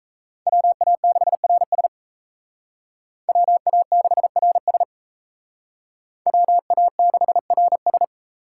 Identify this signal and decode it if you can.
Morse code, heard as a keyed tone, about 35 wpm, 700 Hz. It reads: WA6RS WA6RS WA6RS